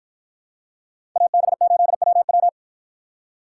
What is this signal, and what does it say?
Morse code, heard as a keyed tone, about 53 wpm, 700 Hz. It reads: AB8WW